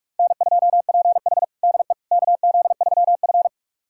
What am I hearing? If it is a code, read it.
Morse code, heard as a keyed tone, about 41 wpm, 700 Hz. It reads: N2PH DE KZ3F
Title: USA Callsign DE Callsign - 4 Characters - 45wpm